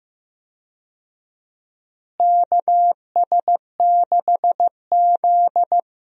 Morse code, heard as a keyed tone, about 15 wpm, 700 Hz. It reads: KS6Z
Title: USA Call Signs - 4 Characters - Repeated 2x - 15wpm